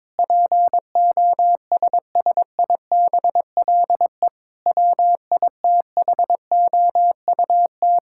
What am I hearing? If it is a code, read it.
Morse code, heard as a keyed tone, about 22 wpm, 700 Hz. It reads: POSSIBLE WITHOUT
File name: practice / 22wpm / Sets of 7 Words - 22wpm